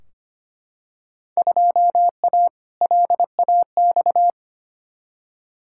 Morse code, heard as a keyed tone, about 25 wpm, 700 Hz. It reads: 2A LAX
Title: Field Day Exchanges - 25wpm